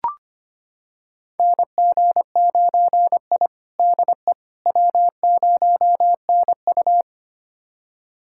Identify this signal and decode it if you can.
Morse code, heard as a keyed tone, about 25 wpm, 700 Hz. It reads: NG9I DE W0NU
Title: USA Callsign DE Callsign - 4 Characters - 25wpm